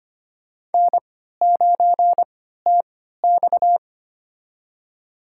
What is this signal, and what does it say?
Morse code, heard as a keyed tone, about 25 wpm, spaced as for 14 wpm, 700 Hz. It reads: N9TX
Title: USA Call Signs - 3x Character Spacing - 25wpm